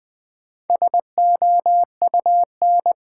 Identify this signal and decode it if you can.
Morse code, heard as a keyed tone, about 20 wpm, 700 Hz. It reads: SOUN
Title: Build A Word - 1 Letter At A Time - 5-Letter Words - 20wpm